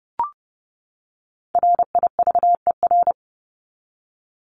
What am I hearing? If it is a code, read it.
Morse code, heard as a keyed tone, about 30 wpm, 700 Hz. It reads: RIVER
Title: Top 500 Words 30wpm